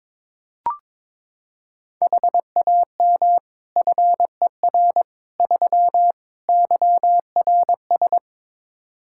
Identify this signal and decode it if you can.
Morse code, heard as a keyed tone, about 22 wpm, 700 Hz. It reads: HAM FER 3 YRS
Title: Practice QSO - Exchange 3 - More Info - 22wpm